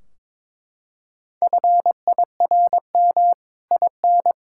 Morse code, heard as a keyed tone, about 21 wpm, 700 Hz. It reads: FIRM IN
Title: Sentences from Top 1600 Words 22wpm